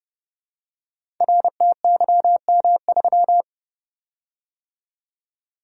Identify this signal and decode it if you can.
Morse code, heard as a keyed tone, about 30 wpm, 700 Hz. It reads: RTYM3
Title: Code Groups - 5 Letter:Numbers - 30wpm